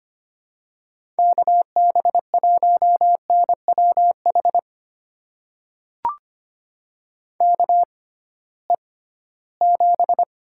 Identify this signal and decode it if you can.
Morse code, heard as a keyed tone, about 25 wpm, spaced as for 16 wpm, 700 Hz. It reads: KB1NWH KE7
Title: USA Call Signs - 6 Characters - 6x Character Spacing - 25wpm